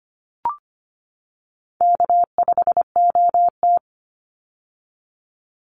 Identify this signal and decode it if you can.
Morse code, heard as a keyed tone, about 25 wpm, 700 Hz. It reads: K5OT
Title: Easy Call Signs - 25wpm